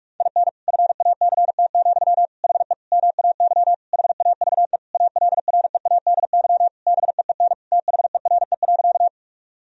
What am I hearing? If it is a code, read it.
Morse code, heard as a keyed tone, about 41 wpm, 700 Hz. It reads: IN FACT, HE MAY HAVE ALREADY BEEN THERE.
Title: Sentences from Top 1000 Words - Review Part 3 - 45wpm